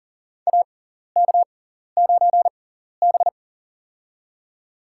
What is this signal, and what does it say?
Morse code, heard as a keyed tone, about 40 wpm, spaced as for 14 wpm, 700 Hz. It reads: AK9B
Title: USA Call Signs - 4 Characters - 6x Character Spacing - 40wpm